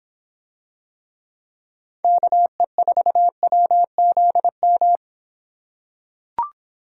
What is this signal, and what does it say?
Morse code, heard as a keyed tone, about 26 wpm, 700 Hz. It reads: KE4WZM